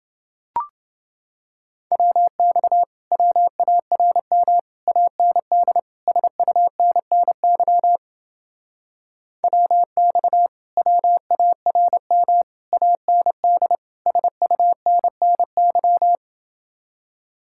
Morse code, heard as a keyed tone, about 29 wpm, 700 Hz. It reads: WX WARM AND SUNNY WX WARM AND SUNNY